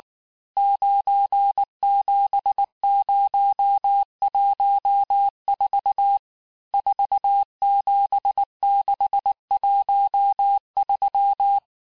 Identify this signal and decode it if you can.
Morse code, heard as a keyed tone, about 19 wpm, 800 Hz. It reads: 97014 47613
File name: cours-du-lundi files / cours-1589-texte-chiffre-vitesse-18mpm